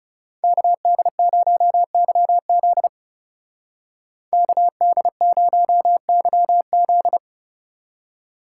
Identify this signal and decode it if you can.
Morse code, heard as a keyed tone, about 37 wpm, 700 Hz. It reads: KD0YZ KD0YZ
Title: Speed Racing - USA 2x2 Call Signs 30wpm